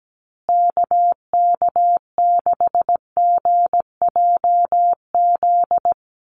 Morse code, heard as a keyed tone, about 17 wpm, 700 Hz. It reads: KK6GJZ
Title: USA Call Signs Repeated 3x 17wpm